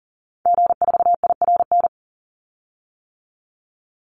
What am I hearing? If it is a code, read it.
Morse code, heard as a keyed tone, about 37 wpm, 700 Hz. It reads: G4IRN
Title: Call Signs - 40wpm